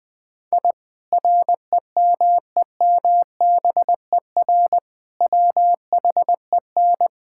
Morse code, heard as a keyed tone, about 21 wpm, 700 Hz. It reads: I REMEMBER WHEN
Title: Sentences with 2-Word Phrases - 20wpm